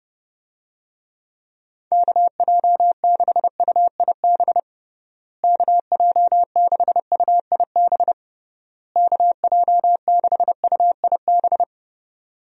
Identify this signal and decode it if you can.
Morse code, heard as a keyed tone, about 30 wpm, 700 Hz. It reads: KJ6UIB KJ6UIB KJ6UIB
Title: USA Call Signs - 6 Characters - Repeated 3x - 30wpm